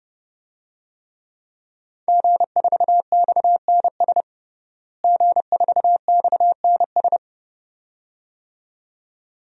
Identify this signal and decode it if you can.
Morse code, heard as a keyed tone, about 30 wpm, 700 Hz. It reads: G4XNS G4XNS
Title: DXCC-100 Call Signs - 5 Characters - Repeated 2x - 30wpm